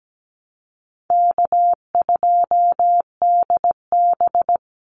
Morse code, heard as a keyed tone, about 17 wpm, 700 Hz. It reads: K2DB
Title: Easy Call Signs Repeated 3x - 17wpm